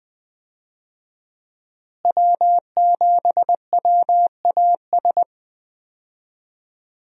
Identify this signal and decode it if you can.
Morse code, heard as a keyed tone, about 20 wpm, 700 Hz. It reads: W7WAS
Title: USA Call Signs - 5 Characters - Repeated 3x - 20wpm